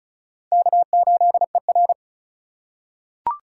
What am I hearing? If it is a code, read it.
Morse code, heard as a keyed tone, about 35 wpm, 700 Hz. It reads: K8ER
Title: USA Call Signs - 3x Character Spacing - 35wpm